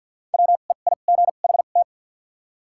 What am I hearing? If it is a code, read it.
Morse code, heard as a keyed tone, about 50 wpm, spaced as for 36 wpm, 700 Hz. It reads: WEIGHT
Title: 2x Character Spacing - Top 500 Words - 50wpm